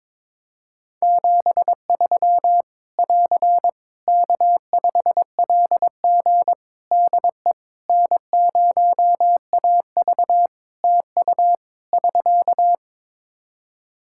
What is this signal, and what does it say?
Morse code, heard as a keyed tone, about 22 wpm, 700 Hz. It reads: 73 <AR> K5LG DE N0AV TU <SK>
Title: K1USN SST QSO Practice 22wpm